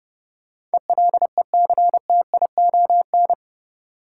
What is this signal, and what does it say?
Morse code, heard as a keyed tone, about 30 wpm, 700 Hz. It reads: ELECTION
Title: Sentences from Top 1500 Words - New Word Drill Alt - 30wpm